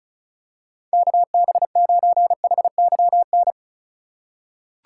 Morse code, heard as a keyed tone, about 36 wpm, 700 Hz. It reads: KB9HYN
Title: USA Call Signs - 2x Character Spacing - 35wpm